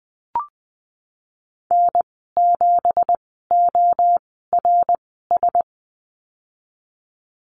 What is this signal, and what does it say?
Morse code, heard as a keyed tone, about 20 wpm, spaced as for 14 wpm, 700 Hz. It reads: N7ORS
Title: 186 Lesson 30g - Letter b - Call Signs 20wpm